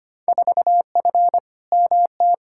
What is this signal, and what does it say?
Morse code, heard as a keyed tone, about 25 wpm, 700 Hz. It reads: 4F MT
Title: Field Day Exchanges Encore - 25wpm